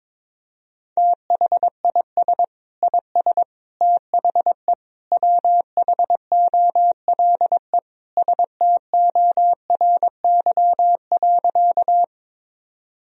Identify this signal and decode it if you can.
Morse code, heard as a keyed tone, about 22 wpm, 700 Hz. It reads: THIS IS THE WHOLE STORY.
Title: Sentences from Top 1000 Words - Review Part 1 - 22wpm